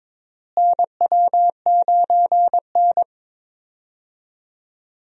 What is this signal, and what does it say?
Morse code, heard as a keyed tone, about 21 wpm, 700 Hz. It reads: NW9N
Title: USA Call Signs - 4 Characters - Repeated 3x - 22wpm